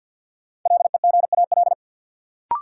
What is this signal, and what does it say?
Morse code, heard as a keyed tone, about 52 wpm, 700 Hz. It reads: LEGAL
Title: Sentences from Top 1200 Words - New Word Drill Alt - 50wpm